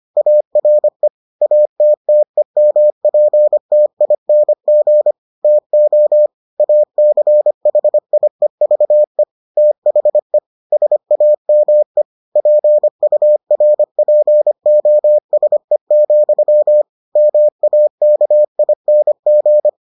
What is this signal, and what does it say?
Morse code, heard as a keyed tone, about 25 wpm, 600 Hz. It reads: ARE ATTEMPTING TO ACHIEVE THE SAME PURPOSE, MAKING